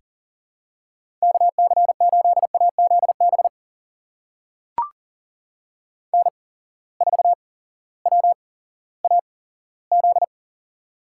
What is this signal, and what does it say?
Morse code, heard as a keyed tone, about 42 wpm, spaced as for 19 wpm, 700 Hz. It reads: KC8AZB N4WAZ
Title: USA Call Signs - 8x Character Spacing - 40wpm